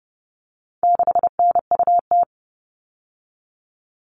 Morse code, heard as a keyed tone, about 30 wpm, 700 Hz. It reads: 6NUT
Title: All Cut Numbers - 4-Digit 30wpm